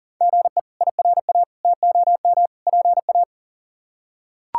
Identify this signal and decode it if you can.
Morse code, heard as a keyed tone, about 40 wpm, 700 Hz. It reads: GE IRA TOM PA
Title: K1USN SST QSO Practice - 40wpm